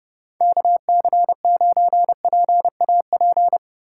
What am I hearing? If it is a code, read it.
Morse code, heard as a keyed tone, about 32 wpm, 700 Hz. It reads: KC9PAP